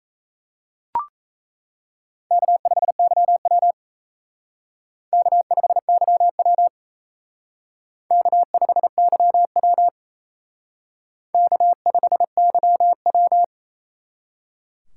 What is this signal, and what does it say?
Morse code, heard as a keyed tone, about 34 wpm, 700 Hz. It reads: K5YW K5YW K5YW K5YW
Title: Speed Racing - USA 1x2 Call Signs 28wpm